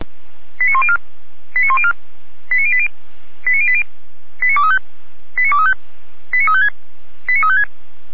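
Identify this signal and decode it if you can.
Five-tone selective call (ZVEI), heard as quick runs of five tones, tones of about 70 ms. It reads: 89104 89104 89080 89080 89235 89235 89356 89356
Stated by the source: ZVEI five-tone call 89092 received on an SDR.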